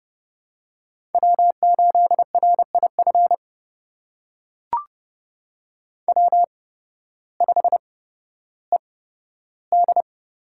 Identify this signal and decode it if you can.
Morse code, heard as a keyed tone, about 30 wpm, spaced as for 14 wpm, 700 Hz. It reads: W8RIF W5ED
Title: USA Call Signs - 5 Characters - 8x Character Spacing - 30wpm